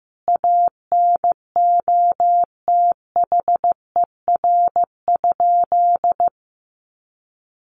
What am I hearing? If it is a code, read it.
Morse code, heard as a keyed tone, about 15 wpm, 700 Hz. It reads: ANOTHER?
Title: Sentences from Top 1400 Words 15wpm